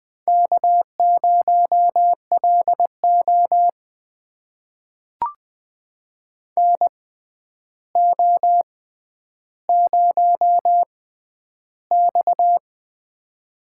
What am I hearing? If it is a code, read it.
Morse code, heard as a keyed tone, about 20 wpm, spaced as for 10 wpm, 700 Hz. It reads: K0LO NO0X
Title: USA Call Signs - 6x Character Spacing - 20wpm